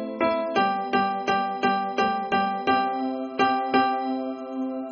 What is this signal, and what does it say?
There is no code.